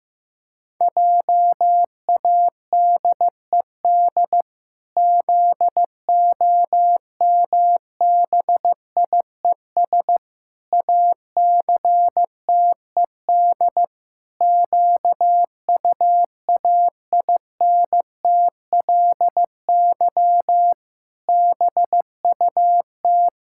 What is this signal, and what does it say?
Morse code, heard as a keyed tone, about 15 wpm, 700 Hz. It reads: JADED ZOMBIES ACTED QUAINTLY BUT
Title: Pangram Challenge v2 - 15wpm